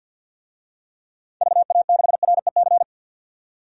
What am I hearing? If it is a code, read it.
Morse code, heard as a keyed tone, about 52 wpm, 700 Hz. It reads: VA6REC
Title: DXCC-100 Call Signs - 6 Characters - 50wpm